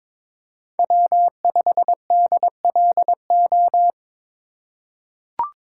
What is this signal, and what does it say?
Morse code, heard as a keyed tone, about 21 wpm, 700 Hz. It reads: W5DLO
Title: USA Call Signs - 3x Character Spacing - 22wpm